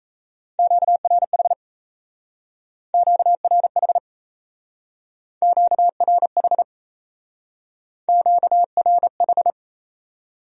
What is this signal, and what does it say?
Morse code, heard as a keyed tone, about 36 wpm, 700 Hz. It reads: QRH QRH QRH QRH